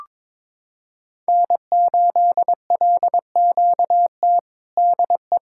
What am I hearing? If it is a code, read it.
Morse code, heard as a keyed tone, about 22 wpm, 700 Hz. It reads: N8LQT DE
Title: K1USN SST QSO Practice 22wpm